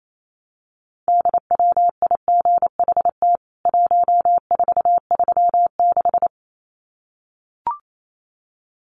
Morse code, heard as a keyed tone, about 28 wpm, 700 Hz. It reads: DWIGHT 1436
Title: CWT Exchange 28wpm